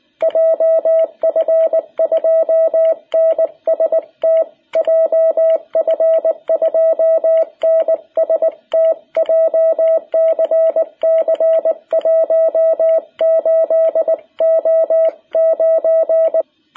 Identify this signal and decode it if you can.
Morse code, heard as a keyed tone, about 19 wpm, 660 Hz. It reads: JF2NST JF2NST JCC 18O9
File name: jh8cbh / 2021cw